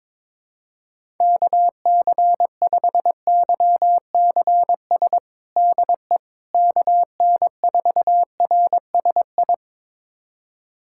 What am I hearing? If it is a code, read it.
Morse code, heard as a keyed tone, about 22 wpm, 700 Hz. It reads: KC5YCS DE KN4RSI
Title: USA Callsign DE Callsign - 6 Characters - 22wpm